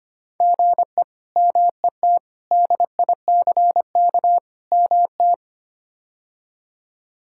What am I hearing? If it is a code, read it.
Morse code, heard as a keyed tone, about 25 wpm, 700 Hz. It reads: GE MET DICK MT
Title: K1USN SST QSO Practice - 25wpm